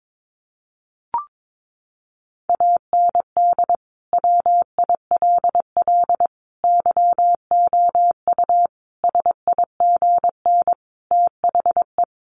Morse code, heard as a keyed tone, about 22 wpm, 700 Hz. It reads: AND WILL YOU SIGN THE
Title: Sentences from Top 1400 Words 22wpm